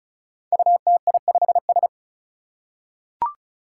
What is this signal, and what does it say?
Morse code, heard as a keyed tone, about 34 wpm, 700 Hz. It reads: UTI5S
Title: Five Challenging Letter-Number Combinations 35wpm